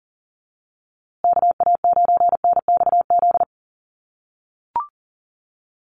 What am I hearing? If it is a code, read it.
Morse code, heard as a keyed tone, about 40 wpm, 700 Hz. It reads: KA9NXZ
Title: USA Call Signs Repeated 2x 40wpm